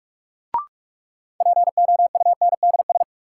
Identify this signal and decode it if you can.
Morse code, heard as a keyed tone, about 47 wpm, 700 Hz. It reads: POUNDS
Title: Sentences from Top 1000 Words - Review Part 2 - New Word Drill - 45wpm